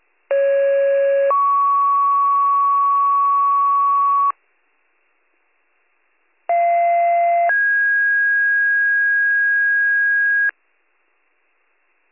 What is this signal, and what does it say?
Two-tone page, heard as a one-second tone, then a three-second tone, about 550 then 1120 Hz, about 690 then 1650 Hz. It is heard from 0.3 to 10.5 s.